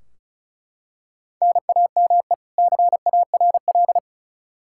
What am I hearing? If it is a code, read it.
Morse code, heard as a keyed tone, about 36 wpm, 700 Hz. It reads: NAME CARL
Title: Practice QSO - Exchange 1 - Beginning - 35wpm